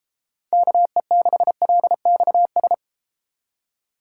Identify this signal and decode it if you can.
Morse code, heard as a keyed tone, about 33 wpm, 700 Hz. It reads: KE6LXS